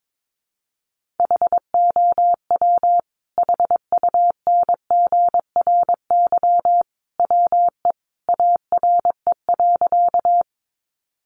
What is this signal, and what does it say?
Morse code, heard as a keyed tone, about 22 wpm, 700 Hz. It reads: HOW HUNGRY WE ARE.
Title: Sentences from Top 1400 Words 22wpm